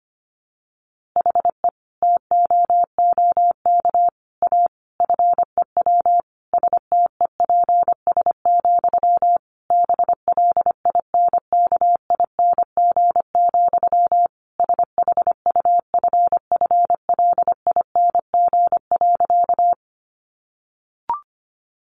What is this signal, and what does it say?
Morse code, heard as a keyed tone, about 25 wpm, 700 Hz. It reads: HE TOOK A FEW STEPS, BLINKING, SHUFFLING.
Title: A Sound Of Thunder 25wpm